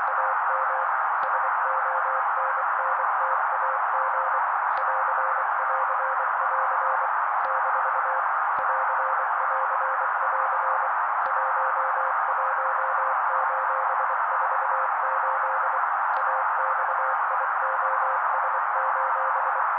Morse code, heard as a keyed tone, about 24 wpm, 530 Hz. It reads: AM SONNTAG <AR><AR><AR> - <AR><AR><AR> 11:48 AXIOS:
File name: daily / 23wpm noise